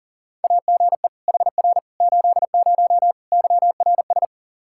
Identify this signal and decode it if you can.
Morse code, heard as a keyed tone, about 40 wpm, 700 Hz. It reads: AGE HR 80 YRS
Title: K1USN SST QSO Practice 40wpm